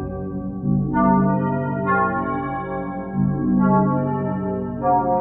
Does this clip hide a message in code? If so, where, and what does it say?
No.